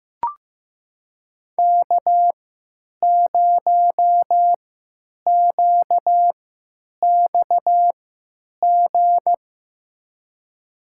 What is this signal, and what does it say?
Morse code, heard as a keyed tone, about 15 wpm, spaced as for 9 wpm, 700 Hz. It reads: K0QXG
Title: USA Call Signs - 3x Character Spacing - 15wpm